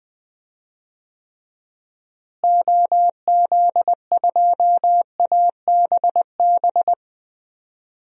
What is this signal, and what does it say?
Morse code, heard as a keyed tone, about 20 wpm, 700 Hz. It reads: OZ2ABB
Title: DXCC-100 Call Signs - 6 Characters - Repeated 3x - 20wpm